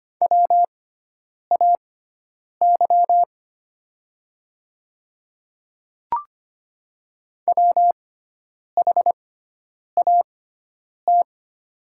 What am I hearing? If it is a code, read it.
Morse code, heard as a keyed tone, about 25 wpm, spaced as for 9 wpm, 700 Hz. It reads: WAY WHAT
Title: 6x Character Spacing - Top 100 Words - 25wpm